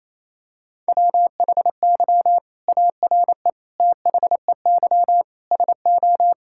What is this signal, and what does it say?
Morse code, heard as a keyed tone, about 28 wpm, 700 Hz. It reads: WHY ARE THEY SO
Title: Sentences from Top 1000 Words - Review Part 5 - 28wpm